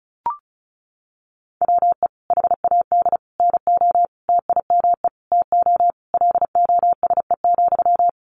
Morse code, heard as a keyed tone, about 34 wpm, 700 Hz. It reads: WE HAD NO TIME TO LOSE,
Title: Sentences from Top 1100 Words 35wpm